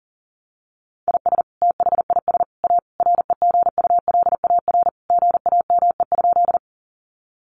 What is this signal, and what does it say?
Morse code, heard as a keyed tone, about 40 wpm, 700 Hz. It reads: IS THIS A REGULAR GAME?
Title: Sentences from Top 900 Words 40wpm